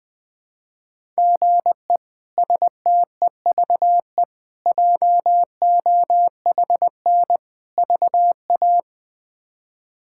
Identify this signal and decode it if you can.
Morse code, heard as a keyed tone, about 20 wpm, 700 Hz. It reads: GE STEVE JOHN VA
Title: K1USN SST QSO Practice - 20wpm